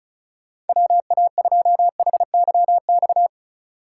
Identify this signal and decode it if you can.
Morse code, heard as a keyed tone, about 36 wpm, 700 Hz. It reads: WA2HYX
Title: USA Call Signs - 6 Characters - 3x Character Spacing - 35wpm